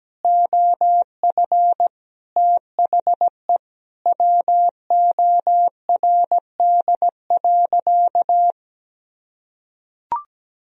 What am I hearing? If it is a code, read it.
Morse code, heard as a keyed tone, about 17 wpm, 700 Hz. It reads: OF THE WORD.
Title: Sentences from Top 1000 Words - Review Part 4 - 17wpm